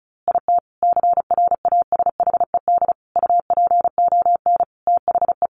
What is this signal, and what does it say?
Morse code, heard as a keyed tone, about 34 wpm, 700 Hz. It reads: IT CRASHED UPON THE
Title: A Sound Of Thunder 35wpm